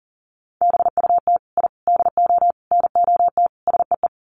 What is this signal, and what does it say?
Morse code, heard as a keyed tone, about 40 wpm, 700 Hz. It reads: BUT I DO NOT SEE
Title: Sets of 5 Words Encore 40wpm